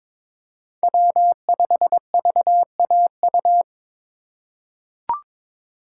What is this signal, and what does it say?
Morse code, heard as a keyed tone, about 22 wpm, 700 Hz. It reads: W5VAU
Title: USA Call Signs - 5 Characters - 4x Character Spacing - 22wpm